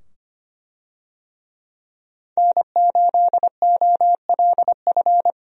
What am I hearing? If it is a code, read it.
Morse code, heard as a keyed tone, about 25 wpm, 700 Hz. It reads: N8OLF
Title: USA Call Signs - 5 Characters - 25wpm